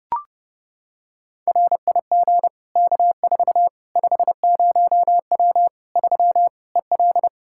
Morse code, heard as a keyed tone, about 30 wpm, 700 Hz. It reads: RIG K4 50W 3 EL
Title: Practice QSO - Exchange 2 - Info - 30wpm